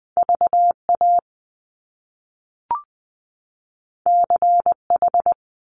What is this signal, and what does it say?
Morse code, heard as a keyed tone, about 20 wpm, 700 Hz. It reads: VA CH
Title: Top Two Letter Combinations in Words 20wpm